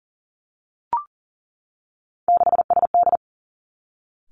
Morse code, heard as a keyed tone, about 42 wpm, 700 Hz. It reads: 6SD
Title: Three Challenging Letter-Number Combinations 40wpm